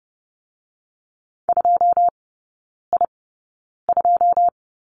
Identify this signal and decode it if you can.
Morse code, heard as a keyed tone, about 30 wpm, 700 Hz. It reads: 2 I 2
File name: 081 Lesson 14e - Number 2 - Three Characters 30wpm